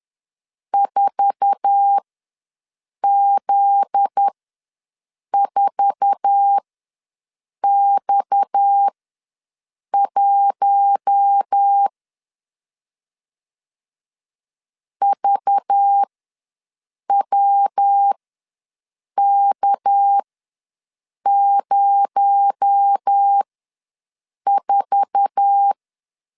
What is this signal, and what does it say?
Morse code, heard as a keyed tone, about 10 wpm, spaced as for 6 wpm, 800 Hz. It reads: 4Z4X1 VWK04
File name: LEZIONI / 8 YJ45 / 8 YJ4 ws 10wpm 4020